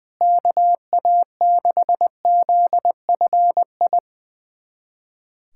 Morse code, heard as a keyed tone, about 20 wpm, 700 Hz. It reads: KA6ZFI
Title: USA Callsign DE Callsign - 6 Characters - 20wpm